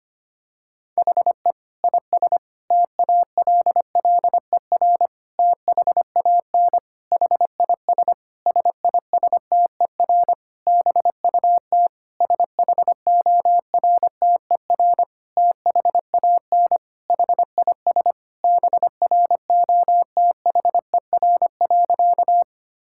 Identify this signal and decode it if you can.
Morse code, heard as a keyed tone, about 25 wpm, 700 Hz. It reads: HE IS TALLER THAN HIS SISTER BUT SHORTER THAN HIS BROTHER.
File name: practice / 25wpm / Difficult Sentences with Hints - 25wpm